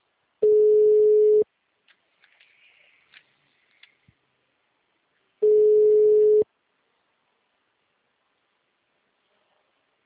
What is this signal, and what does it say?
Ringback tone (Europe, 425 Hz).